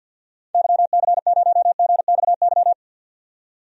Morse code, heard as a keyed tone, about 47 wpm, 700 Hz. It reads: YC0GXY
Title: DXCC-100 Call Signs - 6 Characters - 50wpm